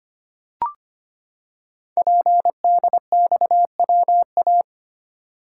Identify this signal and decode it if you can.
Morse code, heard as a keyed tone, about 25 wpm, 700 Hz. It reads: PDXWA
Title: Code Groups - 5 Letter:Numbers - 25wpm